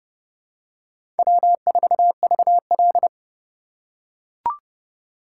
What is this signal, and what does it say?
Morse code, heard as a keyed tone, about 30 wpm, 700 Hz. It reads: W4VL
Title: USA Call Signs - 4 Characters - 4x Character Spacing - 30wpm